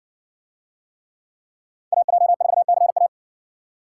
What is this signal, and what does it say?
Morse code, heard as a keyed tone, about 56 wpm, 700 Hz. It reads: AJ4CA